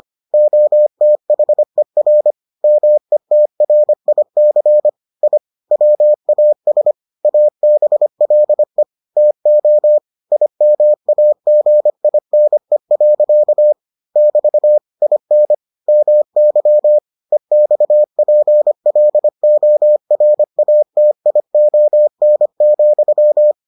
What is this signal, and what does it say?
Morse code, heard as a keyed tone, about 25 wpm, 600 Hz. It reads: OTHER METRIC I WAS ABLE TO IMAGINE. <BT> IN MY EXPLORATION,